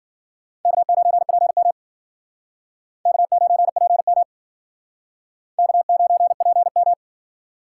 Morse code, heard as a keyed tone, about 51 wpm, 700 Hz. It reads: K9PM K9PM K9PM